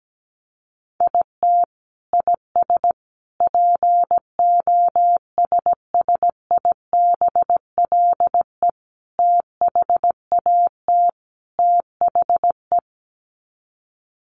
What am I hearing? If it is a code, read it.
Morse code, heard as a keyed tone, about 17 wpm, 700 Hz. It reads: IT IS POSSIBLE THAT THE
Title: Sets of 5 Words Encore2 17wpm